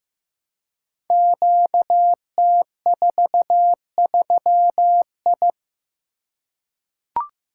Morse code, heard as a keyed tone, about 15 wpm, 700 Hz. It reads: QT43I
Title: Code Groups - 5 Letter:Numbers - 15wpm